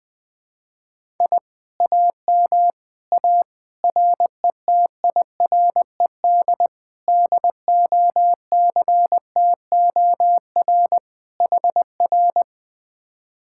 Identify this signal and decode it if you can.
Morse code, heard as a keyed tone, about 20 wpm, 700 Hz. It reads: I AM A RETIRED DOCTOR HR
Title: Speed-Racing Common QSO Phrases Encore - 20wpm